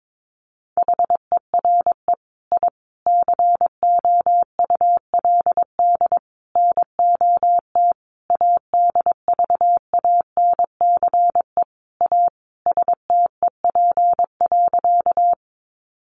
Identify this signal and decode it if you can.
Morse code, heard as a keyed tone, about 22 wpm, 700 Hz. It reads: HERE I COULD NOT ADVANCE A STEP.
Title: Sentences from Top 1400 Words 22wpm